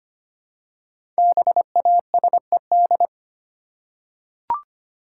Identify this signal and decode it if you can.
Morse code, heard as a keyed tone, about 25 wpm, 700 Hz. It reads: BASED
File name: 5 Letter Words - First 3 Letters Given - 25wpm